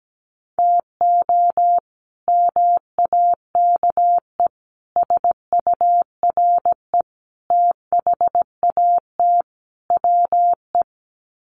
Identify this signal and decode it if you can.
Morse code, heard as a keyed tone, about 17 wpm, 700 Hz. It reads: TO MAKE SURE THAT WE
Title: Sets of 5 Words Encore 17wpm